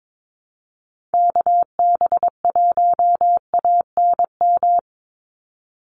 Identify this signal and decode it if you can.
Morse code, heard as a keyed tone, about 22 wpm, 700 Hz. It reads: KB1ANM
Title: USA Call Signs 22wpm